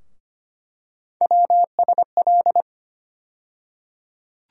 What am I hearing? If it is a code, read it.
Morse code, heard as a keyed tone, about 25 wpm, 700 Hz. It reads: WSL
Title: USA Call Sign Postfixes - 3 Letters 25wpm